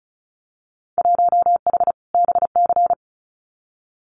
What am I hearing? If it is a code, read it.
Morse code, heard as a keyed tone, about 36 wpm, 690 Hz. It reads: 1H BC
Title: Winter Field Day Exchanges 35wpm